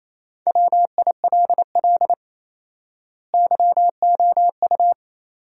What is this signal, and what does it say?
Morse code, heard as a keyed tone, about 28 wpm, 700 Hz. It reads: WILL YOU
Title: 4x Word-Spacing - Sentences from Top 200 Words - 28wpm